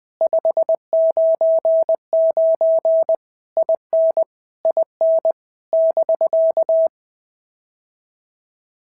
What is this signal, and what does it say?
Morse code, heard as a keyed tone, about 20 wpm, 650 Hz. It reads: 599 IN IN <BK>
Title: POTA QSO Practice - 20wpm